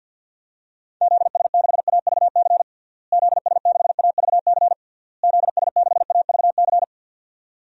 Morse code, heard as a keyed tone, about 52 wpm, 700 Hz. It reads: ZS6AVC ZS6AVC ZS6AVC
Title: DXCC-100 Call Signs - 6 Characters - Repeated 3x - 50wpm